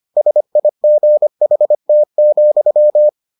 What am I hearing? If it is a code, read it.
Morse code, heard as a keyed tone, about 25 wpm, 600 Hz. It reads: SIGHT,